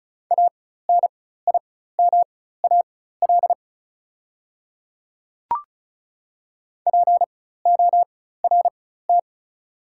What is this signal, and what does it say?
Morse code, heard as a keyed tone, about 34 wpm, spaced as for 16 wpm, 690 Hz. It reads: ANIMAL PORT
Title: 4x Character Spacing - Top 200 Words - 35wpm